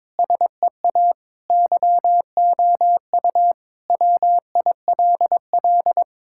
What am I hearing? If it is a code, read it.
Morse code, heard as a keyed tone, about 22 wpm, 700 Hz. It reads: SEA YOU WILL
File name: practice / 22wpm / Sentences with 3-Word Phrases - 22wpm